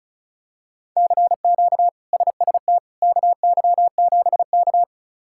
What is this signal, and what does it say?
Morse code, heard as a keyed tone, about 36 wpm, 700 Hz. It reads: CQ SST KY7K
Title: K1USN SST QSO Practice - 35wpm